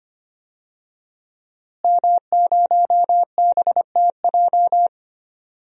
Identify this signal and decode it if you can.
Morse code, heard as a keyed tone, about 25 wpm, 700 Hz. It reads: M0BTJ
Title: DXCC-100 Call Signs - 5 Characters - Repeated 3x - 25wpm